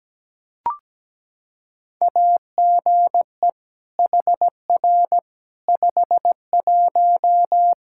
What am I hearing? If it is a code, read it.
Morse code, heard as a keyed tone, about 17 wpm, 700 Hz. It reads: AGE HR 51
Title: Practice QSO - Exchange 3 - More Info - 17wpm